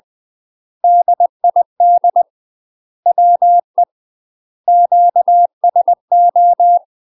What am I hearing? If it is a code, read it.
Morse code, heard as a keyed tone, about 20 wpm, 700 Hz. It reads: DID WE QSO
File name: UK MASTER QSOs TRAINING / 20WPM / UK MASTER QSOs 0052